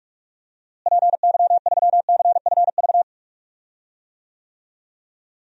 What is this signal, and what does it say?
Morse code, heard as a keyed tone, about 46 wpm, 700 Hz. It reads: PY3CFV
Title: DXCC-100 Call Signs - 6 Characters - Repeated 2x - 45wpm